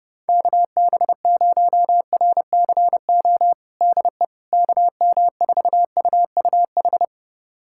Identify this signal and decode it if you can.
Morse code, heard as a keyed tone, about 30 wpm, 700 Hz. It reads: KB0RCO DE KM4UUH
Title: USA Callsign DE Callsign - 6 Characters - 30wpm